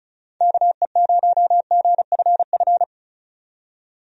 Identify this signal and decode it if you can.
Morse code, heard as a keyed tone, about 34 wpm, 700 Hz. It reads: KE0GFF
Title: USA Call Signs - 6 Characters - 35wpm